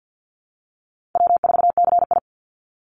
Morse code, heard as a keyed tone, about 50 wpm, 700 Hz. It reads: W4CI
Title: Easy Call Signs - 50wpm